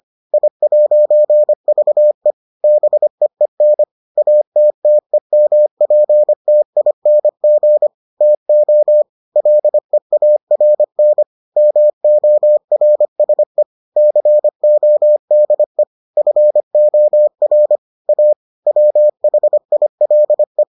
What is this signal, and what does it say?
Morse code, heard as a keyed tone, about 25 wpm, 600 Hz. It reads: I'VE BEEN ATTEMPTING TO LEARN MORSE CODE FOR A WHILE